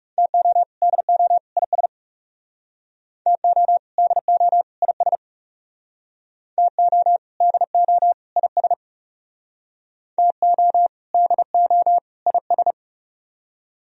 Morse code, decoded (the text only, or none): TO DO IS TO DO IS TO DO IS TO DO IS